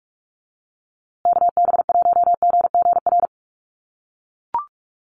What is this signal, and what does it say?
Morse code, heard as a keyed tone, about 44 wpm, 700 Hz. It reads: KB1GGR
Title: USA Call Signs Repeated 3x 45wpm